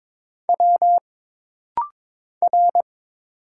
Morse code, heard as a keyed tone, about 22 wpm, 700 Hz. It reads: W R
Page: This rapid-fire timing is geared to push your ICR (Instant Character Recognition) skills.